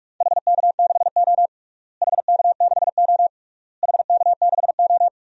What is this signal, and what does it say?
Morse code, heard as a keyed tone, about 41 wpm, 700 Hz. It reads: HK6O HK6O HK6O
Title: DXCC-100 Call Signs - 4 Characters - Repeated 3x - 45wpm